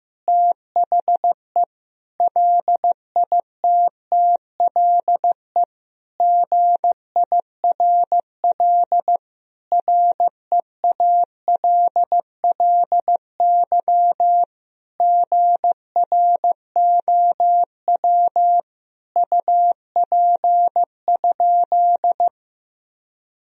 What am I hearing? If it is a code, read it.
Morse code, heard as a keyed tone, about 15 wpm, 700 Hz. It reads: THE LITTLE GIRL REALLY GROW UP?
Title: Sentences from Top 1000 Words - Review Part 1 - 15wpm